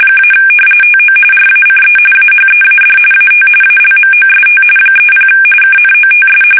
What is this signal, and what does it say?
RTTY, about 45.45 baud. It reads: - TO RACE AHEAD AT ALMOST 30 KNOTS
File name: fsk850 / battleshipnj